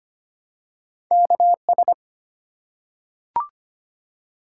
Morse code, heard as a keyed tone, about 25 wpm, 700 Hz. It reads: KS
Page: morse-code-ninja-practice-25wpm